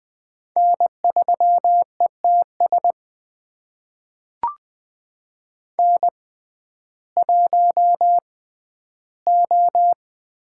Morse code, heard as a keyed tone, about 20 wpm, spaced as for 12 wpm, 700 Hz. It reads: N3ETS N1O
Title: USA Call Signs - 5 Characters - 6x Character Spacing - 20wpm